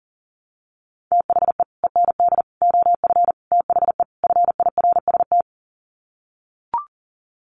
Morse code, heard as a keyed tone, about 40 wpm, 700 Hz. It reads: THE END OF THE FIRST
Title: Sets of 5 Words Encore 40wpm